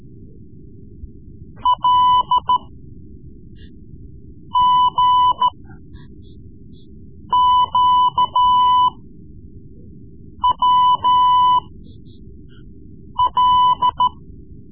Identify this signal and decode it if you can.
Morse code, heard as a keyed tone, about 13 wpm, spaced as for 4 wpm, 1000 Hz. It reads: LGQWL
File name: Morse Lessons / Morse Lessons / Tape-3B